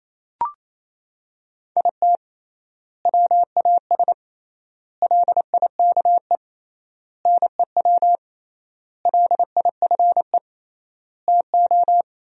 Morse code, heard as a keyed tone, about 28 wpm, 700 Hz. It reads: IT WAS LIKE NEW LIFE TO
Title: 3x Word-Spacing - Sentences from Top 300 Words - 28wpm